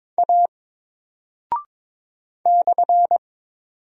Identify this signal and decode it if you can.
Morse code, heard as a keyed tone, about 22 wpm, 700 Hz. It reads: A /
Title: Single Character (Letter:Number:Punctuation:Prosign) Alt 22wpm